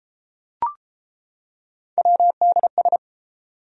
Morse code, heard as a keyed tone, about 31 wpm, 700 Hz. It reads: WDS